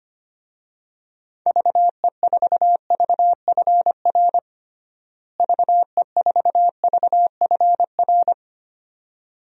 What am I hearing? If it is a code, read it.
Morse code, heard as a keyed tone, about 25 wpm, 700 Hz. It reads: VE4VFR VE4VFR
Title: DXCC-100 Call Signs - 6 Characters - Repeated 2x - 25wpm